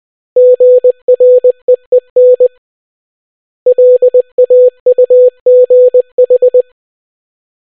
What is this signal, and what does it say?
Morse code, heard as a keyed tone, about 20 wpm, 500 Hz. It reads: GREEN LAUGH